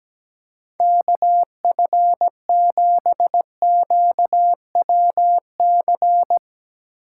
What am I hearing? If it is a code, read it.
Morse code, heard as a keyed tone, about 17 wpm, 700 Hz. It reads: KF7QWC
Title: USA Call Signs - 4x Character Spacing - 17wpm